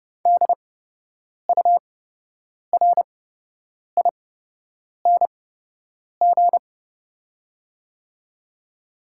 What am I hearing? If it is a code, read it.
Morse code, heard as a keyed tone, about 30 wpm, spaced as for 8 wpm, 700 Hz. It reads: DURING
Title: 8x Character Spacing - Top 500 Words - 30wpm